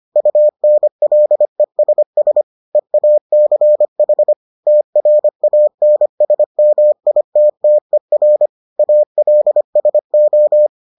Morse code, heard as a keyed tone, about 25 wpm, 600 Hz. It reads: UNLESS EACH TRANSMITTER ALSO